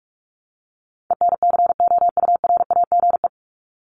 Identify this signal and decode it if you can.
Morse code, heard as a keyed tone, about 44 wpm, 700 Hz. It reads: ENCOURAGE
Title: Top 1251-1500 Words 45wpm